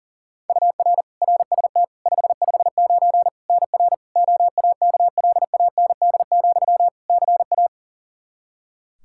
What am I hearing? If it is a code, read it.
Morse code, heard as a keyed tone, about 40 wpm, 700 Hz. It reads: UR RST 559 NR OAKLAND, CA
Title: K1USN SST QSO Practice 40wpm